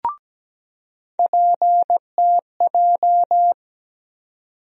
Morse code, heard as a keyed tone, about 17 wpm, 700 Hz. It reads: PTJ
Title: USA Call Sign Postfixes - 3 Letters 17wpm